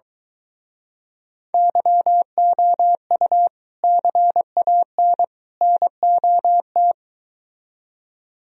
Morse code, heard as a keyed tone, about 23 wpm, 700 Hz. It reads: YOU CAN NOT